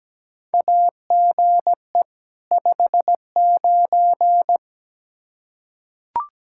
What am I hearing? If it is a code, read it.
Morse code, heard as a keyed tone, about 17 wpm, 700 Hz. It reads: AGE 59
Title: Practice QSO - Exchange 3 - More Info - 17wpm